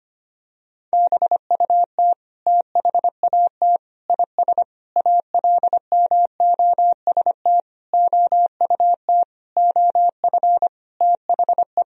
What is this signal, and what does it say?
Morse code, heard as a keyed tone, about 25 wpm, 700 Hz. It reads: BUT THAT IS ALMOST OUT OF THE
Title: Sentences from Top 1000 Words - Review Part 4 - 25wpm